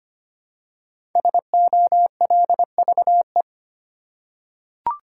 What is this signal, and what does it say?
Morse code, heard as a keyed tone, about 25 wpm, 700 Hz. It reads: SOLVE
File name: 5 Letter Words - First 3 Letters Given - 25wpm